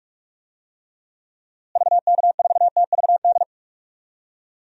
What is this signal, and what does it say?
Morse code, heard as a keyed tone, about 42 wpm, 700 Hz. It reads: VK4TVD
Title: DXCC-100 Call Signs - 6 Characters - Repeated 2x - 45wpm